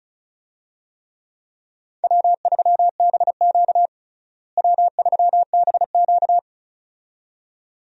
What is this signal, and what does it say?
Morse code, heard as a keyed tone, about 36 wpm, 700 Hz. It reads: W3BQ W3BQ
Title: USA Call Signs - 4 Characters - Repeated 2x - 35wpm